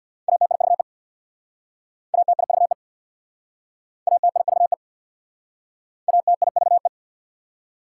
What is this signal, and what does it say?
Morse code, heard as a keyed tone, about 64 wpm, 690 Hz. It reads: ATIVE ATIVE ATIVE ATIVE